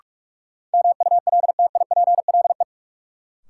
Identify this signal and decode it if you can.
Morse code, heard as a keyed tone, about 46 wpm, 700 Hz. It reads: MULTIPLE
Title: Sets of 5 Words - New Word Drill - 45wpm